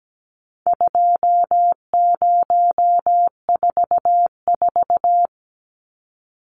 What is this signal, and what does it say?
Morse code, heard as a keyed tone, about 17 wpm, 700 Hz. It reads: 2044
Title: Numbers - 4-Digit 17wpm